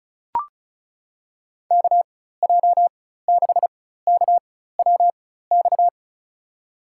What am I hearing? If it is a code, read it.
Morse code, heard as a keyed tone, about 36 wpm, spaced as for 16 wpm, 690 Hz. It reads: KJ6KWX
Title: USA Call Signs - 6 Characters - 4x Character Spacing - 35wpm